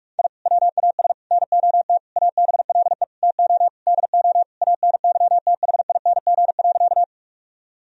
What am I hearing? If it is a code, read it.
Morse code, heard as a keyed tone, about 46 wpm, 700 Hz. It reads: I WAS NOT ABLE TO DO ANYTHING.
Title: Sentences from Top 1000 Words - Review Part 2 - 45wpm